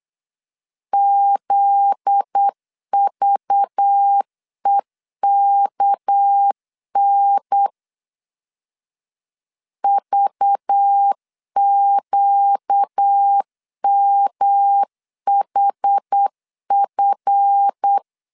Morse code, heard as a keyed tone, about 8 wpm, 800 Hz. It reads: ZVEKN VQMHF
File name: LEZIONI / 7 GF9Z / 7 GF9Z 8wpm 3213